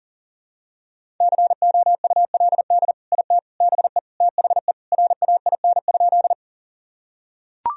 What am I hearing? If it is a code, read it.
Morse code, heard as a keyed tone, about 37 wpm, 700 Hz. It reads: COULD IT BE THE RAIN?
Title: Sentences from Top 1000 Words - Review Part 1 - 40wpm